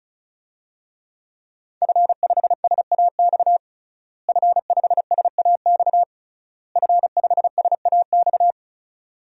Morse code, heard as a keyed tone, about 34 wpm, 700 Hz. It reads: F5SAX F5SAX F5SAX
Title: DXCC-100 Call Signs - 5 Characters - Repeated 3x - 35wpm